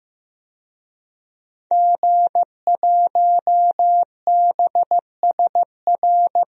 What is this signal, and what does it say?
Morse code, heard as a keyed tone, about 15 wpm, 700 Hz. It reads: G1BSR
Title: DXCC-100 Call Signs - 5 Characters - Repeated 3x - 15wpm